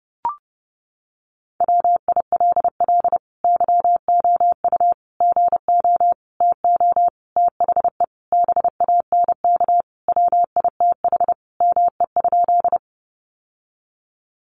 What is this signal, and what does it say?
Morse code, heard as a keyed tone, about 30 wpm, 700 Hz. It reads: WILL YOU GO TO THE BANK WITH ME?
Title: Sentences from Top 900 Words 30wpm